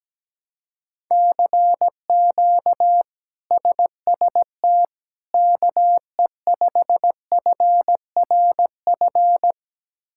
Morse code, heard as a keyed tone, about 17 wpm, 700 Hz. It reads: CQ SST KE5FRF
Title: K1USN SST QSO Practice - 17wpm